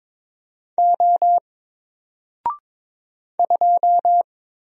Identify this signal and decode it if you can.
Morse code, heard as a keyed tone, about 22 wpm, 700 Hz. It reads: O 2
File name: Single-Letter-Number-22wpm